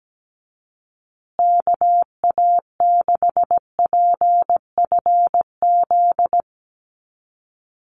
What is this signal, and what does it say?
Morse code, heard as a keyed tone, about 17 wpm, 700 Hz. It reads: KA6PFZ
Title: USA Call Signs Repeated 2x 17wpm